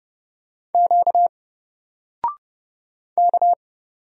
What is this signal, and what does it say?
Morse code, heard as a keyed tone, about 31 wpm, 700 Hz. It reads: Q K
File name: Single-Letter-Number-30wpm